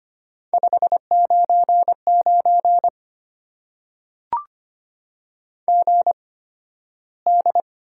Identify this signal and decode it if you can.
Morse code, heard as a keyed tone, about 25 wpm, spaced as for 13 wpm, 690 Hz. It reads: 599 GD
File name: Top 90 QSO Elements - 8x Character Spacing - 25wpm